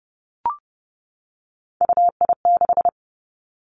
Morse code, heard as a keyed tone, about 30 wpm, 700 Hz. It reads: UI6
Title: Three Challenging Letter-Number Combinations 30wpm